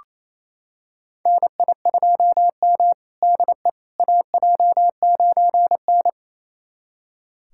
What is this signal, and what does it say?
Morse code, heard as a keyed tone, about 27 wpm, 700 Hz. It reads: NI2M DE AJ9N
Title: USA Callsign DE Callsign - 4 Characters - 28wpm